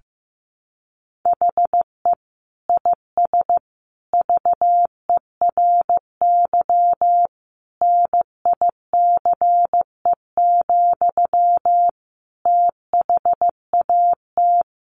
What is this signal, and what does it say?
Morse code, heard as a keyed tone, about 15 wpm, 700 Hz. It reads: HE IS VERY NICE, THAT
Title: Sentences from Top 1100 Words 15wpm